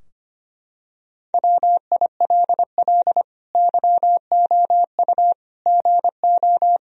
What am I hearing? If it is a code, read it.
Morse code, heard as a keyed tone, about 25 wpm, 700 Hz. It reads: WILL YOU GO
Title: Sentences from Top 1000 Words - Review Part 3 - 25wpm